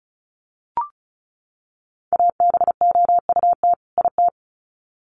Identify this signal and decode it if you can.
Morse code, heard as a keyed tone, about 34 wpm, 700 Hz. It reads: ABOUT IT
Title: Sets of 2 Words 35wpm